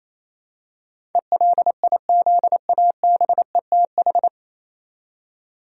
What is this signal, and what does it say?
Morse code, heard as a keyed tone, about 28 wpm, 700 Hz. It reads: ELIZABETH
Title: USA Common Names - 28wpm